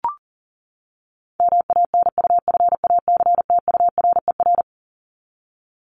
Morse code, heard as a keyed tone, about 40 wpm, 700 Hz. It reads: MANUFACTURER
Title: Top 1751-2000 Words 40wpm